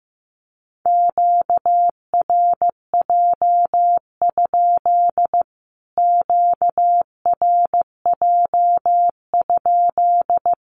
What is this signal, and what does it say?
Morse code, heard as a keyed tone, about 15 wpm, 700 Hz. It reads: QRJ? QRJ?
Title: Q-Codes Repeated 3x 15wpm